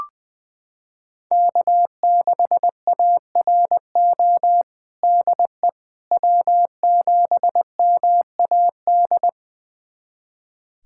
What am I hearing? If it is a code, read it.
Morse code, heard as a keyed tone, about 20 wpm, 700 Hz. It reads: K6ARO DE W7MAD
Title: USA Callsign DE Callsign - 5 Characters - 20wpm